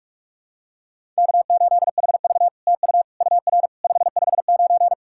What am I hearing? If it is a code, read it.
Morse code, heard as a keyed tone, about 44 wpm, 700 Hz. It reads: K8HV TU UR 559
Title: SOTA QSO Practice 45wpm